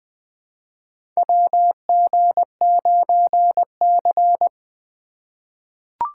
Morse code, heard as a keyed tone, about 20 wpm, 700 Hz. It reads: WG9C
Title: USA Call Signs - 4 Characters - 4x Character Spacing - 20wpm